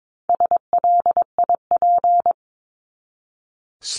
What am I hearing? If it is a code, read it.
Morse code, heard as a keyed tone, about 22 wpm, 700 Hz. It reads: SLIP
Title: Top 1751-2000 Words 22wpm